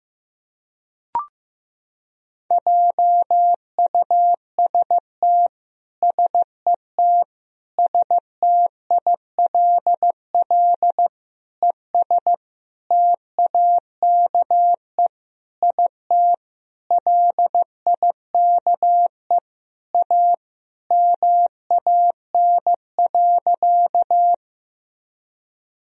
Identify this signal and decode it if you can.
Morse code, heard as a keyed tone, about 15 wpm, 700 Hz. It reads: JUST SET STILL ES TAKE IT LIKE A MAN.
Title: Sentences from Top 300 Words and Abbreviations 15wpm